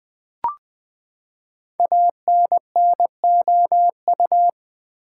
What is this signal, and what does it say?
Morse code, heard as a keyed tone, about 20 wpm, 700 Hz. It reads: ANNOU
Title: Build A Word - 5 Letters At A Time - 12-Letter Words - 20wpm